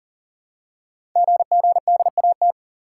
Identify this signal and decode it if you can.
Morse code, heard as a keyed tone, about 42 wpm, 700 Hz. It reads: GGDAT
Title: Code Groups - 5 Letters - 40wpm